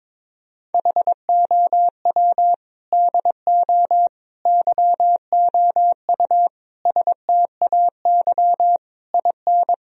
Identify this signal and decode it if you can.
Morse code, heard as a keyed tone, about 22 wpm, 700 Hz. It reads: HOW DO YOU STAY IN
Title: Sentences from Top 1000 Words - Review Part 3 - 22wpm